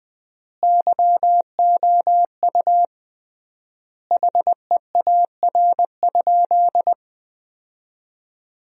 Morse code, heard as a keyed tone, about 20 wpm, 700 Hz. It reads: YOU HEAR?
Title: 3x Word-Spacing - Sentences from Top 300 Words - 20wpm